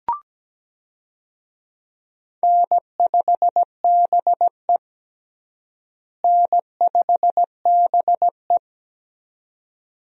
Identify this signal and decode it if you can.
Morse code, heard as a keyed tone, about 17 wpm, 700 Hz. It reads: N5BE N5BE
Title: USA Call Signs - 4 Characters - Repeated 2x - 17wpm